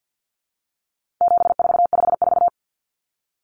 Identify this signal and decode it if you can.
Morse code, heard as a keyed tone, about 49 wpm, 700 Hz. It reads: 7454
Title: Numbers - 4-Digit 50wpm